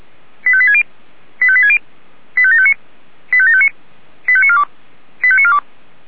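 Five-tone selective call (ZVEI), heard as quick runs of five tones, tones of about 70 ms. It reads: 85860 85860 85759 85759 85942 85942
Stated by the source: ZVEI five-tone call received on an SDR.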